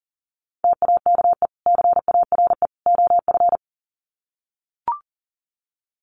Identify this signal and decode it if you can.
Morse code, heard as a keyed tone, about 40 wpm, 700 Hz. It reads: TAKE CARE OF
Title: Sets of 3 Words 40wpm